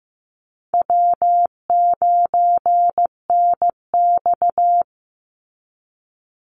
Morse code, heard as a keyed tone, about 15 wpm, 700 Hz. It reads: W9NX
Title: Easy Call Signs - 15wpm